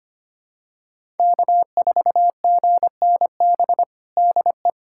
Morse code, heard as a keyed tone, about 25 wpm, 700 Hz. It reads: K4GNB DE
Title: USA Callsign DE Callsign - 5 Characters - 25wpm